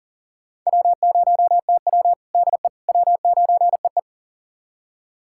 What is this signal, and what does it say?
Morse code, heard as a keyed tone, about 39 wpm, 700 Hz. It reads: W0TW DE W9EE
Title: USA Callsign DE Callsign - 4 Characters - 40wpm